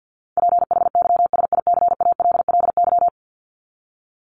Morse code, heard as a keyed tone, about 50 wpm, 700 Hz. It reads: PHYSICALLY
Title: Top 1501-1750 Words 50wpm